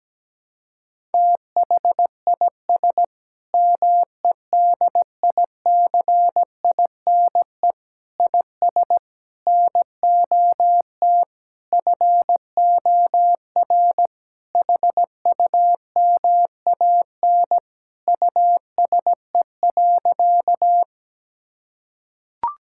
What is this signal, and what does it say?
Morse code, heard as a keyed tone, about 17 wpm, 700 Hz. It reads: THIS MEDICINE IS NOT FOR HUMAN USE.
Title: Sentences from Top 1000 Words - Review Part 6 - 17wpm